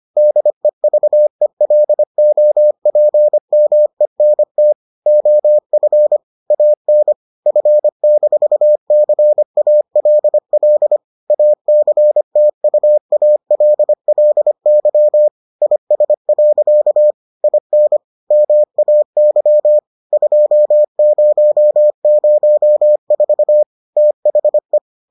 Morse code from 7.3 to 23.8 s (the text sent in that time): F-CALL ACTUALLY IS. IN MAY 2004